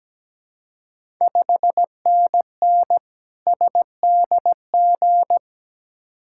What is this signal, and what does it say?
Morse code, heard as a keyed tone, about 17 wpm, 700 Hz. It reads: 5NN SDG
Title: Ontario QSO Party Exchange 17wpm